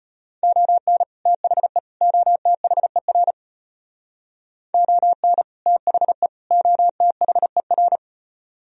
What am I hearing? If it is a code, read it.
Morse code, heard as a keyed tone, about 33 wpm, 700 Hz. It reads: ON THE OTHER ON THE OTHER